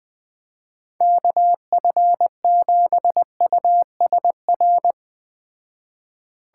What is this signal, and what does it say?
Morse code, heard as a keyed tone, about 20 wpm, 700 Hz. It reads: KF7USR